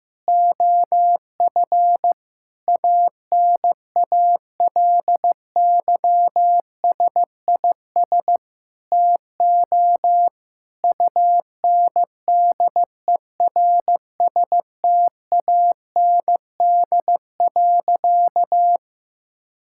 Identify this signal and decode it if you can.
Morse code, heard as a keyed tone, about 16 wpm, 700 Hz. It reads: OF ANALYSIS TO UNDERSTAND.
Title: Sentences from Top 1000 Words - Review Part 5 - 15wpm